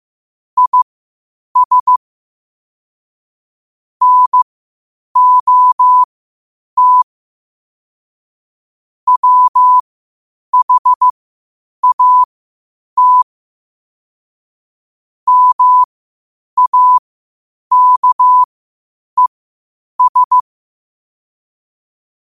Morse code, heard as a keyed tone, about 15 wpm, spaced as for 8 wpm, 1000 Hz. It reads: IS NOT WHAT MAKES
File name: qotdfiles / 8wpm